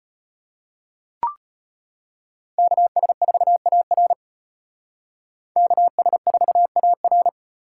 Morse code, heard as a keyed tone, about 33 wpm, 700 Hz. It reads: KS4AR KS4AR